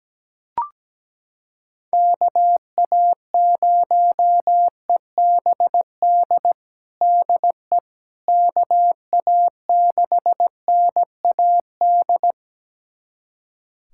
Morse code, heard as a keyed tone, about 17 wpm, 700 Hz. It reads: KA0EBD DE KA6NAD
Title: USA Callsign DE Callsign - 6 Characters - 17wpm